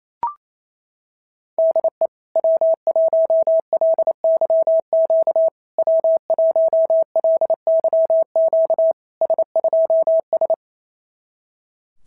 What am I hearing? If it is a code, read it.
Morse code, heard as a keyed tone, about 28 wpm, 650 Hz. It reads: DE W1LYQ W1LYQ S2S
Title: SOTA QSO Practice 28wpm